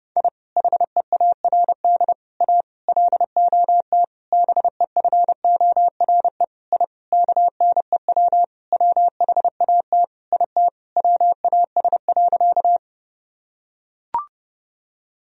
Morse code, heard as a keyed tone, about 30 wpm, 700 Hz. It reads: I HEARD A LOT BEFORE I KNEW WHAT IT WAS.
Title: Sentences from Top 1000 Words - Review Part 3 - 30wpm